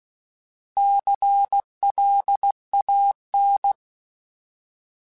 Morse code, heard as a keyed tone, about 16 wpm, 800 Hz. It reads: CLAN
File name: Alphabetical / Alphabetical mp3s / Cs @15